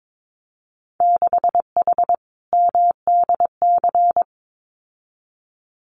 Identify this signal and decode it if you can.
Morse code, heard as a keyed tone, about 22 wpm, 700 Hz. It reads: 6H MDC
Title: Winter Field Day Exchanges Encore 22wpm